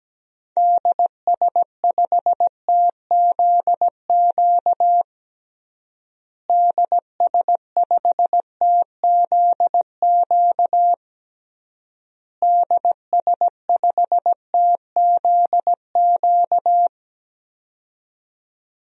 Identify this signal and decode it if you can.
Morse code, heard as a keyed tone, about 17 wpm, 700 Hz. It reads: DS5TZQ DS5TZQ DS5TZQ
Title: DXCC-100 Call Signs - 6 Characters - Repeated 3x - 17wpm